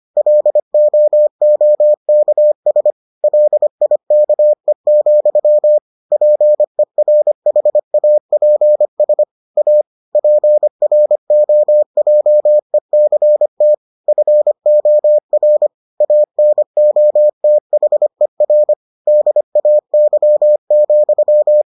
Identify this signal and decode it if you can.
Morse code, heard as a keyed tone, about 25 wpm, 600 Hz. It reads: LOOKS LIKE, PERHAPS A PROJECT FOR ANOTHER DAY,